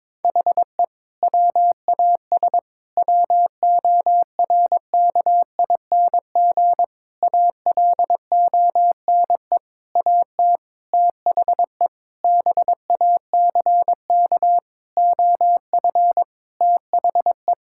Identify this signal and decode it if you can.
Morse code, heard as a keyed tone, about 22 wpm, 700 Hz. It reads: HE WAS WORKING ALONE AT THE BACK OF THE
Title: Sentences from Top 1000 Words - Review Part 5 - 22wpm